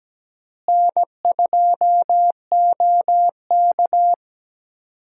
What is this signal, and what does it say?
Morse code, heard as a keyed tone, about 17 wpm, 700 Hz. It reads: N2OK
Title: USA Call Signs - 4 Characters - Repeated 3x - 17wpm